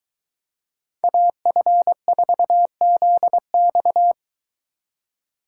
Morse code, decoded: AF4ZX

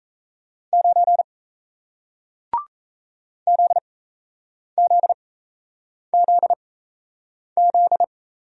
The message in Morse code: 9 ZZZZ

